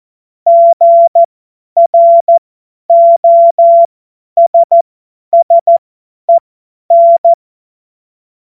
Morse code, heard as a keyed tone, about 14 wpm, spaced as for 10 wpm, 680 Hz. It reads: GROSSEN
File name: Morsekurs / Witze / Witz 07.November 2023 50 F70